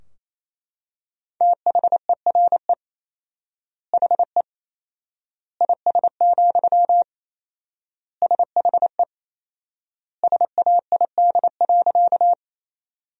Morse code, heard as a keyed tone, about 27 wpm, 700 Hz. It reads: THERE HE IS, SHE SAID.
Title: 4x Word-Spacing - Sentences from Top 100 Words - 28wpm